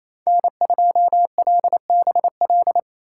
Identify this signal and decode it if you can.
Morse code, heard as a keyed tone, about 28 wpm, 700 Hz. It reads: N2LBL
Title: USA Call Signs - 5 Characters - 4x Character Spacing - 28wpm